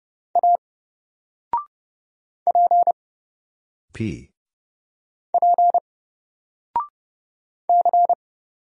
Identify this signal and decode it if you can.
Morse code, heard as a keyed tone, about 30 wpm, 700 Hz. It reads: A P P C
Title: Single Letter-Number Alt 30wpm